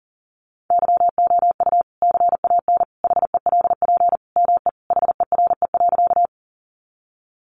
Morse code, heard as a keyed tone, about 40 wpm, 700 Hz. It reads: YOU CAN HELP ME HERE.
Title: Sentences from Top 200 Words 40wpm